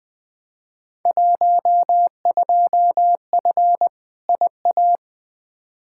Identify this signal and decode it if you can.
Morse code, heard as a keyed tone, about 20 wpm, 700 Hz. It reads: 12F IA
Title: Field Day Exchanges Encore - 20wpm